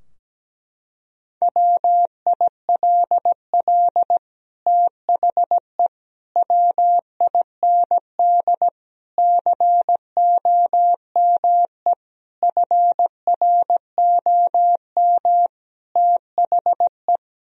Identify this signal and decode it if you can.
Morse code, heard as a keyed tone, about 17 wpm, 700 Hz. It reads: WILL THE WIND COME FROM THE
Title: Sentences from Top 1000 Words - Review Part 2 - 17wpm